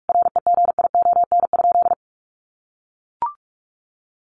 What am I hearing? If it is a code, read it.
Morse code, heard as a keyed tone, about 44 wpm, 700 Hz. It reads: REGION?
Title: Sentences from Top 1500 Words 45wpm